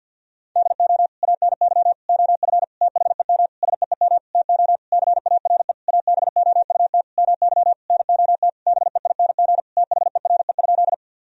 Morse code, heard as a keyed tone, about 50 wpm, 700 Hz. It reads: DO ANY OF THEM SEEM TO CARE ABOUT MY NOT BEING THERE?
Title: Sentences from Top 1000 Words - Review Part 3 - 50wpm